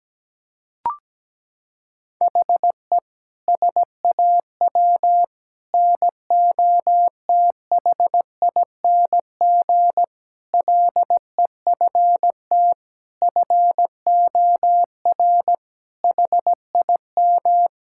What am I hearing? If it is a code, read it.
Morse code, heard as a keyed tone, about 17 wpm, 700 Hz. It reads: HE SAW NOTHING LEFT FOR HIM
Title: Sentences from Top 1600 Words 17wpm